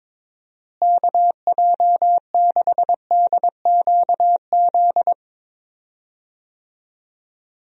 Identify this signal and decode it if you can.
Morse code, heard as a keyed tone, about 22 wpm, 700 Hz. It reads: KJ6DQZ
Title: USA Call Signs - 6 Characters - Repeated 2x - 22wpm